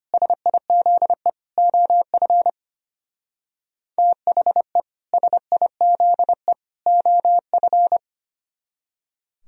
Morse code, heard as a keyed tone, about 33 wpm, 700 Hz. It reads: SIZE OF THE SIZE OF